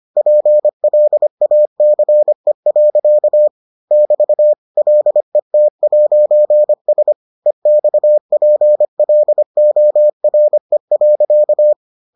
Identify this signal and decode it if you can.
Morse code, heard as a keyed tone, about 25 wpm, 600 Hz. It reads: PLACE. <BT> LET'S EXPLORE.